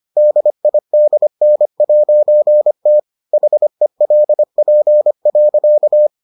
Morse code, heard as a keyed tone, about 25 wpm, 600 Hz. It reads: DIDN'T HELP.